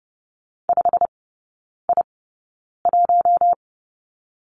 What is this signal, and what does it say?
Morse code, heard as a keyed tone, about 30 wpm, spaced as for 9 wpm, 700 Hz. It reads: 5I1
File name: Lesson 15e - Number 5 - Three Characters 30wpm